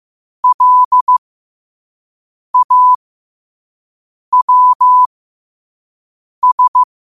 Morse code, heard as a keyed tone, about 15 wpm, spaced as for 5 wpm, 1000 Hz. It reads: LAWS
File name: qotdfiles / 5wpm